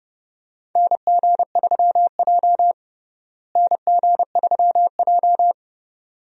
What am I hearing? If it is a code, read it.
Morse code, heard as a keyed tone, about 32 wpm, 700 Hz. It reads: NG3J NG3J
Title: USA Call Signs - 4 Characters - Repeated 2x - 30wpm